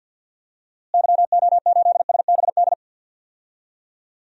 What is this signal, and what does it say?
Morse code, heard as a keyed tone, about 53 wpm, 700 Hz. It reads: YO8SBD
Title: DXCC-100 Call Signs - 6 Characters - Repeated 3x - 50wpm